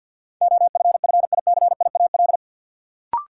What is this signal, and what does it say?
Morse code, heard as a keyed tone, about 52 wpm, 700 Hz. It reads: OFFICIAL
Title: Sentences from Top 1000 Words - New Word Drill - 50wpm